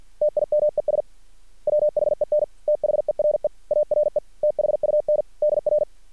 Morse code, heard as a keyed tone, about 50 wpm, 600 Hz. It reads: TIMES WHEN THERE ARE THUNDR
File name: archived / CRSnet